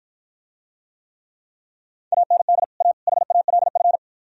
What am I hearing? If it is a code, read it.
Morse code, heard as a keyed tone, about 52 wpm, 700 Hz. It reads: AND A HALF